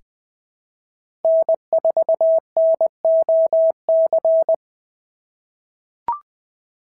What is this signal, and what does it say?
Morse code, heard as a keyed tone, about 20 wpm, 690 Hz. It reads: N4NOC
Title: SOTA QSO Practice 20wpm